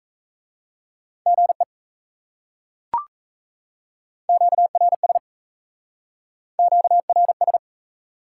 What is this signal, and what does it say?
Morse code, heard as a keyed tone, about 36 wpm, 700 Hz. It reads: GE QRS QRS